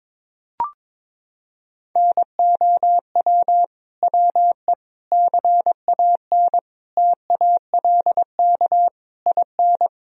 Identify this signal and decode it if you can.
Morse code, heard as a keyed tone, about 22 wpm, 700 Hz. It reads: NOW WE CAN TALK IN
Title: Sentences from Top 1000 Words - Review Part 4 - 22wpm